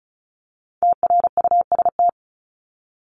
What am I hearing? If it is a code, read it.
Morse code, heard as a keyed tone, about 34 wpm, 700 Hz. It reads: TRUST
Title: Top 1751-2000 Words 35wpm